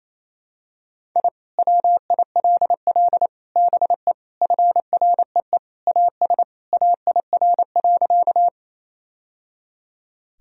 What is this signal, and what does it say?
Morse code, heard as a keyed tone, about 28 wpm, 700 Hz. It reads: I WILL BE FREE AS AIR.
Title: Sentences from Top 1000 Words - Review Part 1 - 28wpm